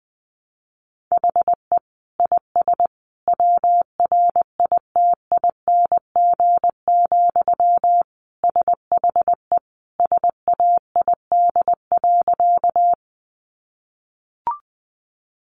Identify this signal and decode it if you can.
Morse code, heard as a keyed tone, about 20 wpm, 700 Hz. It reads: HE IS WRITING, SHE SAID.
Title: Sentences from Top 800 Words 20wpm